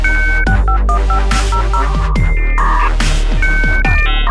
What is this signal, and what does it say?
DTMF, heard as telephone touch tones, tones of about 100 ms. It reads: 651500